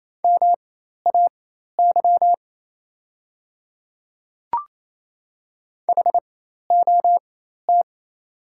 Morse code, heard as a keyed tone, about 28 wpm, spaced as for 13 wpm, 690 Hz. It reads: MAY HOT
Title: 4x Character Spacing - Top 100 Words - 28wpm